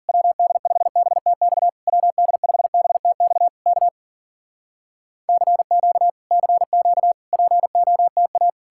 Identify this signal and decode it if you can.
Morse code, heard as a keyed tone, about 50 wpm, 700 Hz. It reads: WD5BTX WD5BTX K CQ CQ POTA